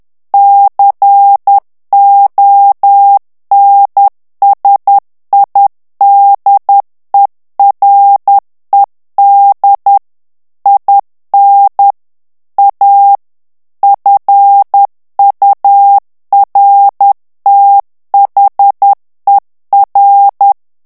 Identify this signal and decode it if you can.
Morse code, heard as a keyed tone, about 10 wpm, 800 Hz. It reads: CONSIDERED IN A FURTHER
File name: ANGLAIS texte clair 3 vitesse 10 mots minute